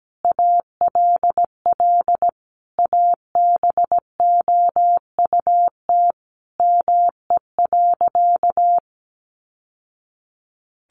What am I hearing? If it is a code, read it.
Morse code, heard as a keyed tone, about 17 wpm, 700 Hz. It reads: ALL ABOUT ME.
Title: Sentences from Top 1200 Words 17wpm